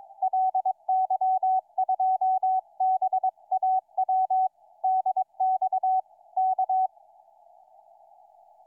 Morse code, heard as a keyed tone, about 22 wpm, 750 Hz. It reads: LY2BAW DX K
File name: ly2baw 28-11-13 2003 160m